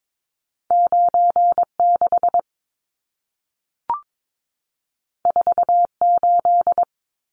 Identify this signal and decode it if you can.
Morse code, heard as a keyed tone, about 22 wpm, 700 Hz. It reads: 96 48
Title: Numbers - 2-Digit 22wpm